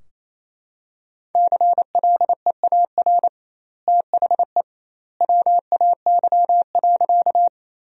Morse code, heard as a keyed tone, about 28 wpm, 700 Hz. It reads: CLEAR THE WAY.
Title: 2x Word-Spacing - Sentences from Top 500 Words - 28wpm